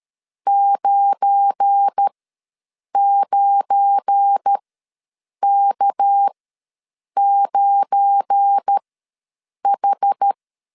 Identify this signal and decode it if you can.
Morse code, heard as a keyed tone, about 13 wpm, spaced as for 7 wpm, 800 Hz. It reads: 99K9H
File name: LEZIONI / 7 GF9Z / 7 GF9 ws 12wpm 3344